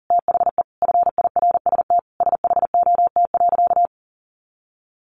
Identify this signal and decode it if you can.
Morse code, heard as a keyed tone, about 40 wpm, 700 Hz. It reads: THE FIRST SHOT.
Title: A Sound Of Thunder 40wpm